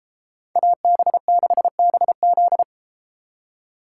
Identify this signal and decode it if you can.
Morse code, heard as a keyed tone, about 32 wpm, 700 Hz. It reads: AB6BZ